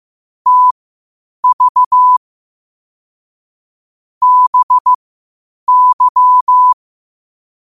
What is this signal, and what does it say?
Morse code, heard as a keyed tone, about 15 wpm, spaced as for 8 wpm, 1000 Hz. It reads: TV BY
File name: qotdfiles / 8wpm